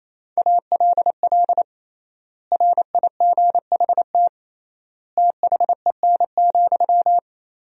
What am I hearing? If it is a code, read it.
Morse code, heard as a keyed tone, about 28 wpm, 700 Hz. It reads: ALL RIGHT THEN,
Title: 3x Word-Spacing - Sentences from Top 500 Words - 28wpm